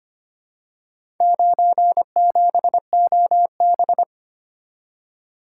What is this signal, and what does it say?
Morse code, heard as a keyed tone, about 25 wpm, 700 Hz. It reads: 97OB
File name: practice / 25wpm / Four Letter-Number Combinations - 25wpm